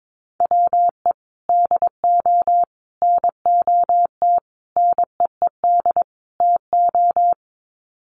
Morse code, heard as a keyed tone, about 22 wpm, 700 Hz. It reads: WE DO NOT NEED TO
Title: Sets of 5 Words Encore 22wpm